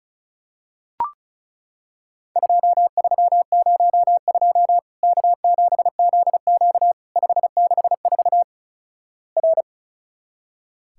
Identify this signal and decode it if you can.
Morse code, heard as a keyed tone, about 36 wpm, 700 Hz. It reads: 2302 K7ZQ 564 R
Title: Worked All Europe DX Contest - QTC Exchange - 35wpm